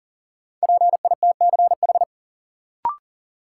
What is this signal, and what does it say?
Morse code, heard as a keyed tone, about 40 wpm, 700 Hz. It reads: PITCH
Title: Sentences from Top 1300 Words - New Word Drill - 40wpm